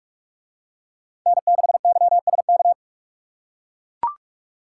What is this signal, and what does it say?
Morse code, heard as a keyed tone, about 41 wpm, 700 Hz. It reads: N6YSK